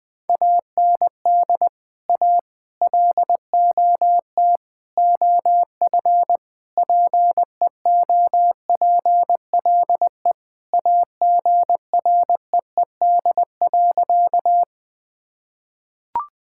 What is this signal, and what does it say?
Morse code, heard as a keyed tone, about 20 wpm, 700 Hz. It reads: AND A LOT OF PEOPLE AGREED.
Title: Sentences with 5-Word Phrases - 20wpm